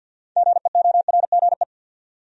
Morse code, heard as a keyed tone, about 52 wpm, 700 Hz. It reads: GEORGE
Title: USA Common Names - 50wpm